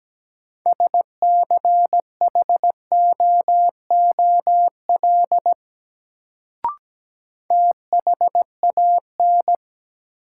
Morse code, heard as a keyed tone, about 17 wpm, 700 Hz. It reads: SCHOOL THAN
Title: Sets of 2 Words - New Word Drill - 17wpm